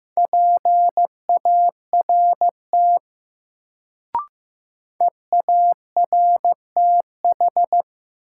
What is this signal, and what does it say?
Morse code, heard as a keyed tone, about 15 wpm, 700 Hz. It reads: PART EARTH
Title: Sentences from Top 200 Words - New Word Drill - 15wpm